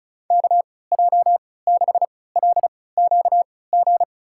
Morse code, heard as a keyed tone, about 36 wpm, spaced as for 20 wpm, 700 Hz. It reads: KJ6LQG
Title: USA Call Signs - 6 Characters - 3x Character Spacing - 35wpm